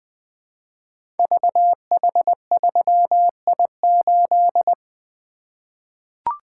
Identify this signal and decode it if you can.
Morse code, heard as a keyed tone, about 20 wpm, 700 Hz. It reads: VH3I8
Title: Code Groups - 5 Letter:Numbers - 20wpm